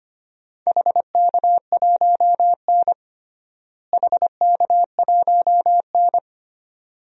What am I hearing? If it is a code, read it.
Morse code, heard as a keyed tone, about 25 wpm, 700 Hz. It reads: HK1N HK1N
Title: DXCC-100 Call Signs - 4 Characters - Repeated 2x - 25wpm